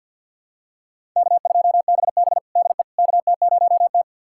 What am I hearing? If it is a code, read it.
Morse code, heard as a keyed tone, about 52 wpm, 700 Hz. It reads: K2BB DE KT0T
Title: USA Callsign DE Callsign - 4 Characters - 50wpm